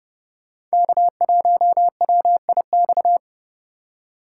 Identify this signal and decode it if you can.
Morse code, heard as a keyed tone, about 38 wpm, 700 Hz. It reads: K1WIX